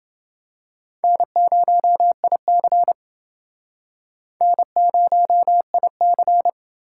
Morse code, heard as a keyed tone, about 29 wpm, 700 Hz. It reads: N0IC N0IC